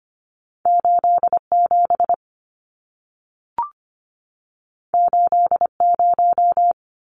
Morse code, heard as a keyed tone, about 25 wpm, 700 Hz. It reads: 87 80
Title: Numbers - 2-Digit 25wpm